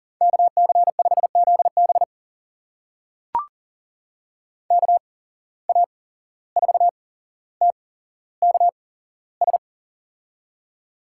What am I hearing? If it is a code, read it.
Morse code, heard as a keyed tone, about 40 wpm, spaced as for 16 wpm, 700 Hz. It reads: KC5ZB KA4TKS
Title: USA Call Signs - 8x Character Spacing - 40wpm